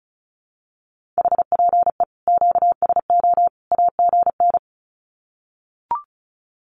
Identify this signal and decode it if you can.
Morse code, heard as a keyed tone, about 34 wpm, 700 Hz. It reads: HPE QSO AGN
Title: Common QSO Phrases 35wpm